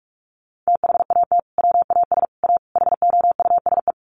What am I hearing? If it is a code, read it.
Morse code, heard as a keyed tone, about 44 wpm, 700 Hz. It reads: THAT WAS A HOUSE
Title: Sentences from Top 200 Words 45wpm